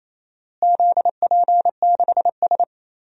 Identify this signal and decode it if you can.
Morse code, heard as a keyed tone, about 28 wpm, 700 Hz. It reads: ZP6S
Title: DXCC-100 Call Signs - 4 Characters - Repeated 3x - 28wpm